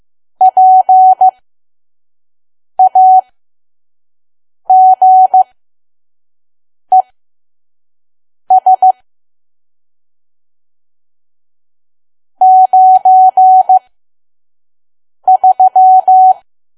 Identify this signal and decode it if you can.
Morse code, heard as a keyed tone, about 15 wpm, spaced as for 5 wpm, 750 Hz. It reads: PAGES 93
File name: Archive / 5 WPM files / PM